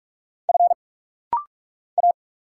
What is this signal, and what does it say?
Morse code, heard as a keyed tone, about 46 wpm, 700 Hz. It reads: F A
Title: Single Letter-Number Alt - Rapid-Fire 45wpm